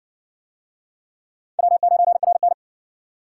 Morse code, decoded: W9RN

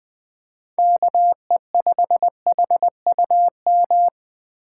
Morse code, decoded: KE5HUM

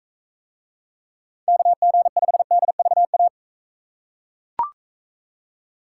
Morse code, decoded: KG5DVA

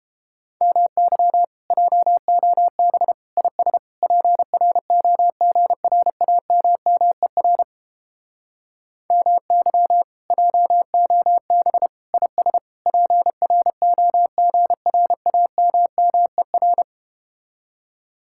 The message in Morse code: MY JOB IS PROGRAMMER MY JOB IS PROGRAMMER

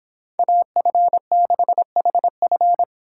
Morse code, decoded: AF6HF

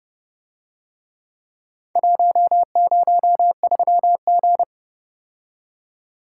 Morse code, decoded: 103G

